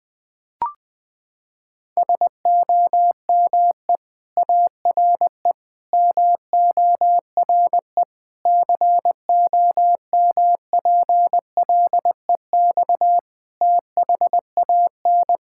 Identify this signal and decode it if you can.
Morse code, heard as a keyed tone, about 20 wpm, 700 Hz. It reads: SOME ARE MORE COMPLEX THAN
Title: Sentences from Top 1000 Words - Review Part 6 - 20wpm